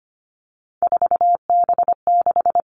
Morse code, heard as a keyed tone, about 25 wpm, 700 Hz. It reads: 4B6
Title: Three Challenging Letter-Number Combinations 25wpm